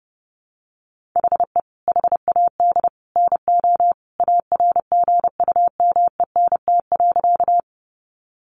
Morse code, heard as a keyed tone, about 30 wpm, 700 Hz. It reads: HE HAD NO ARGUMENT.
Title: Sentences from Top 1400 Words 30wpm